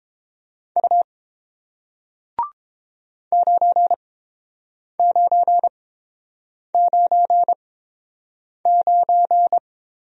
Morse code, heard as a keyed tone, about 30 wpm, spaced as for 7 wpm, 700 Hz. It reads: U 9999